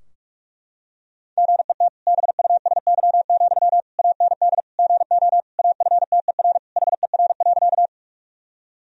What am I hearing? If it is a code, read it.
Morse code, heard as a keyed tone, about 41 wpm, 700 Hz. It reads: GET BUSY, AND GO AFTER HER.
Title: Sentences from Top 1000 Words - Review Part 3 - 45wpm